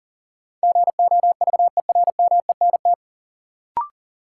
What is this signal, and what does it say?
Morse code, heard as a keyed tone, about 40 wpm, 700 Hz. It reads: GOVERMENT
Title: Sentences from Top 1300 Words - New Word Drill - 40wpm